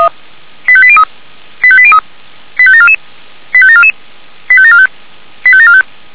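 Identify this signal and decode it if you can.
Five-tone selective call (ZVEI), heard as quick runs of five tones, tones of about 70 ms. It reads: 85803 85803 85744 85744 85745 85745
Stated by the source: ZVEI five-tone selective call 85726 received by SDR, from the Feuerwehr Orlamünde fire brigade server.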